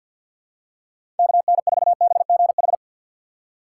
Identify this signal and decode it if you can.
Morse code, heard as a keyed tone, about 53 wpm, 700 Hz. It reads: KN4BGH
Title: USA Call Signs - 6 Characters - 6x Character Spacing - 50wpm